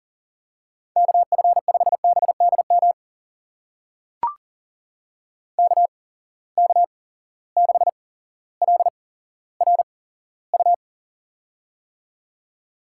Morse code, heard as a keyed tone, about 40 wpm, spaced as for 17 wpm, 700 Hz. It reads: KF5BDM KK6LRU
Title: USA Call Signs - 6 Characters - 8x Character Spacing - 40wpm